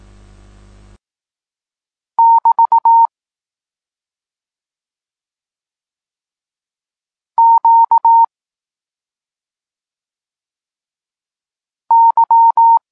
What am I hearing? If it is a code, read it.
Morse code, heard as a keyed tone, about 18 wpm, 920 Hz. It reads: <BT> Q Y
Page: Tekst 32 – Doskonalenie odbioru poznanych liter